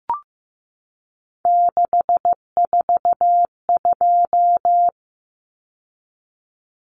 Morse code, heard as a keyed tone, about 15 wpm, 700 Hz. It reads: 642
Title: Numbers - 3-Digit 15wpm